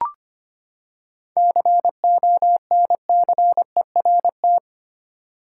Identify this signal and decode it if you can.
Morse code, heard as a keyed tone, about 25 wpm, 700 Hz. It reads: CONCERT
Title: Seven Letter Words - 25wpm